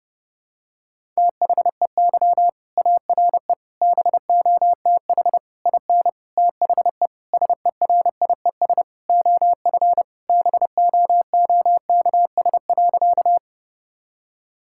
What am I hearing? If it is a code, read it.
Morse code, heard as a keyed tone, about 30 wpm, 700 Hz. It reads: THEY ARE BOTH IN THE SERIES OF BOOKS.
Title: Sentences from Top 1000 Words - Review Part 5 - 30wpm